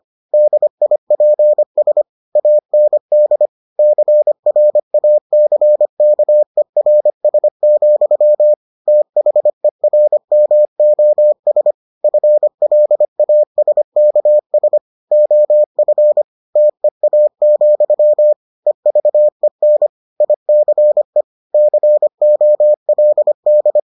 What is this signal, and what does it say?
Morse code, heard as a keyed tone, about 25 wpm, 600 Hz. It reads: DIPS AND CRACKERS, THERMOS FLASKS OF TEA, EVEN ICE COLD